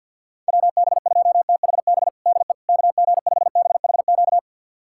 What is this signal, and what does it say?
Morse code, heard as a keyed tone, about 50 wpm, 700 Hz. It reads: WB2THB DE KG5BHQ
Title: USA Callsign DE Callsign 50wpm